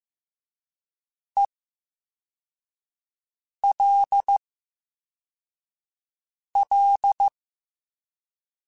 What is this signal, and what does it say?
Morse code, heard as a keyed tone, about 15 wpm, spaced as for 4 wpm, 800 Hz. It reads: ELL
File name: Special INGs / Special INGs-mp3s / Slow ing @13